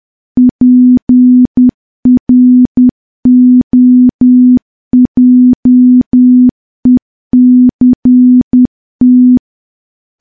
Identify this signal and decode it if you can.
Morse code, heard as a keyed tone, about 10 wpm, 250 Hz. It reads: PROJECT
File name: morse / morse-training-10WPM